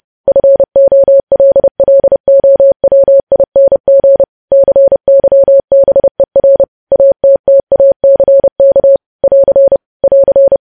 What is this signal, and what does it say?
Morse code, heard as a keyed tone, about 30 wpm, 550 Hz. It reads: FOLLOWING CYBER ATTACK <AR> <AR>
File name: podcast / News headlines-1x-30x30